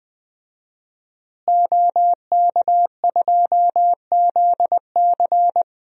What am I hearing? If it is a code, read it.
Morse code, heard as a keyed tone, about 20 wpm, 700 Hz. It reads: OK2ZC
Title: DXCC-100 Call Signs - 5 Characters - Repeated 2x - 20wpm